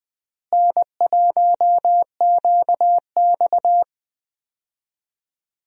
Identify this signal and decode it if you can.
Morse code, heard as a keyed tone, about 20 wpm, 700 Hz. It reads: N1QX
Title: USA Call Signs - 4 Characters - Repeated 3x - 20wpm